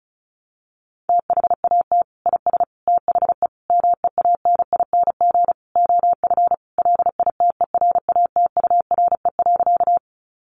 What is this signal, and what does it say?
Morse code, heard as a keyed tone, about 34 wpm, 700 Hz. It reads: THAT IS THE MEANING OF LITERATURE.
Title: Sentences from Top 900 Words 35wpm